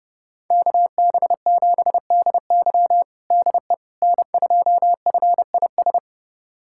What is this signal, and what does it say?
Morse code, heard as a keyed tone, about 30 wpm, 700 Hz. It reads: KB7DY DE N2FIS
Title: USA Callsign DE Callsign - 5 Characters - 30wpm